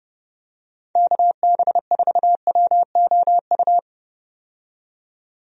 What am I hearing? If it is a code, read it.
Morse code, heard as a keyed tone, about 30 wpm, 700 Hz. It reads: KB4WOU
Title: USA Call Signs - 6 Characters - 30wpm